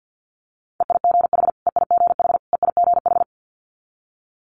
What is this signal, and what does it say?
Morse code, heard as a keyed tone, about 50 wpm, 700 Hz. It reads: EIGH EIGH EIGH
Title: Multi-letter Phonemes Repeated 3x 50wpm